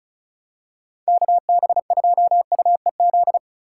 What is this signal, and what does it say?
Morse code, heard as a keyed tone, about 36 wpm, 700 Hz. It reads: KB2UEZ
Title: USA Call Signs - 8x Character Spacing - 35wpm